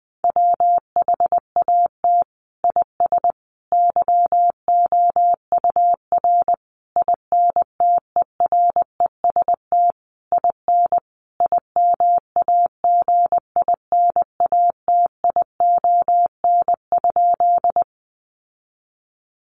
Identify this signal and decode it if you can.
Morse code, heard as a keyed tone, about 20 wpm, 700 Hz. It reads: WHAT IS YOUR INTEREST IN IMAGINATION?
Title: Sentences from Top 1300 Words 20wpm